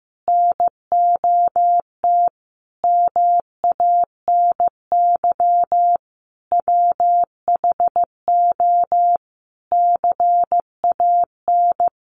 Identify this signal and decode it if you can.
Morse code, heard as a keyed tone, about 15 wpm, 700 Hz. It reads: NOT MANY WHO CAN
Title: Sentences from Top 1300 Words 15wpm